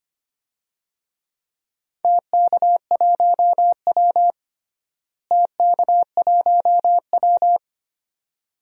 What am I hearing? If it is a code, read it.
Morse code, heard as a keyed tone, about 25 wpm, 700 Hz. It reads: TK1W TK1W
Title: DXCC-100 Call Signs - 4 Characters - Repeated 2x - 25wpm